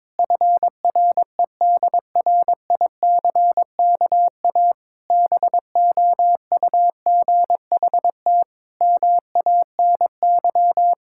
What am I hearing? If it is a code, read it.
Morse code, heard as a keyed tone, about 22 wpm, 700 Hz. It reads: FREDRICKA BOUGHT MANY
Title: Pangram Challenge v2 - 22wpm